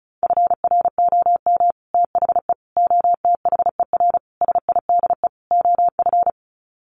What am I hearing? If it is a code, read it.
Morse code, heard as a keyed tone, about 34 wpm, 700 Hz. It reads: FROM THE OTHER SIDE OF
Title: Sets of 5 Words Encore 35wpm